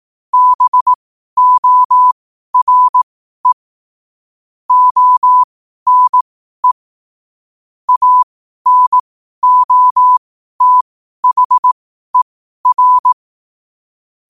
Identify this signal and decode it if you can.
Morse code, heard as a keyed tone, about 18 wpm, spaced as for 12 wpm, 1000 Hz. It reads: BORE ONE ANOTHER
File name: qotdfiles / 12wpm